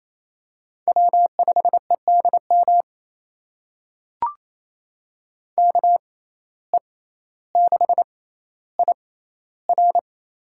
Morse code, heard as a keyed tone, about 27 wpm, spaced as for 14 wpm, 700 Hz. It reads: W5EDM KE6IR
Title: USA Call Signs - 5 Characters - 6x Character Spacing - 28wpm